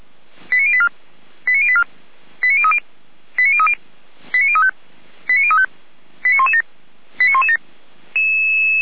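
Five-tone selective call (ZVEI), heard as quick runs of five tones, tones of about 70 ms. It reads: 89084 89084 89030 89030 89035 89035 89117 89117
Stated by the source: ZVEI five-tone selective call received by SDR.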